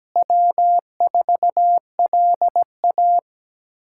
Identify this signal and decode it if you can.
Morse code, heard as a keyed tone, about 17 wpm, 700 Hz. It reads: W4LA
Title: USA Callsign DE Callsign - 4 Characters - 17wpm